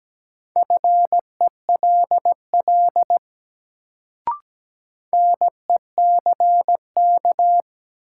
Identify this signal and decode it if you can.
Morse code, heard as a keyed tone, about 17 wpm, 700 Hz. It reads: FELL NECK
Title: Sets of 5 Words Encore - New Word Drill - 17wpm